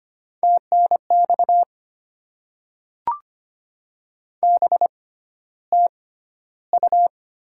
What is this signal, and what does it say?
Morse code, heard as a keyed tone, about 25 wpm, spaced as for 13 wpm, 690 Hz. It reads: TNX BTU
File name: Top 20 QSO Elements - 6x Character Spacing - 25wpm